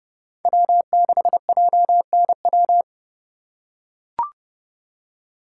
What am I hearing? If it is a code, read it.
Morse code, heard as a keyed tone, about 29 wpm, 700 Hz. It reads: W6JNW